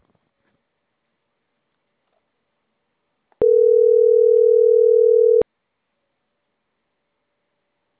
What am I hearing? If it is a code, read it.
Ringback tone (North America), 440 + 480 Hz.